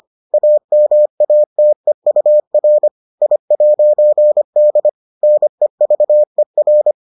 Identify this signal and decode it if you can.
Morse code, heard as a keyed tone, about 25 wpm, 600 Hz. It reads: AMATEUR I'D NEVER